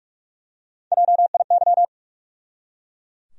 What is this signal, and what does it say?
Morse code, heard as a keyed tone, about 46 wpm, 700 Hz. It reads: JIY
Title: USA Call Sign Postfixes - 3 Letters 45wpm